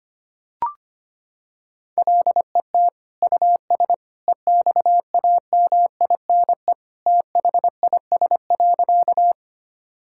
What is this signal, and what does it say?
Morse code, heard as a keyed tone, about 25 wpm, 700 Hz. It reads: LET US EXAMINE THIS.
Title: Sentences from Top 1600 Words 25wpm